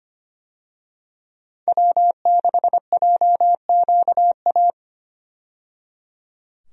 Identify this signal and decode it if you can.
Morse code, heard as a keyed tone, about 25 wpm, 690 Hz. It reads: W6JQA
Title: USA Call Signs - 5 Characters - 25wpm